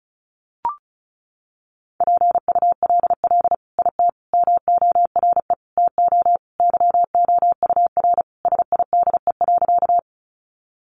Morse code, heard as a keyed tone, about 34 wpm, 700 Hz. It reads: PULL IT MORE TO YOUR SIDE.
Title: Sentences from Top 400 Words 35wpm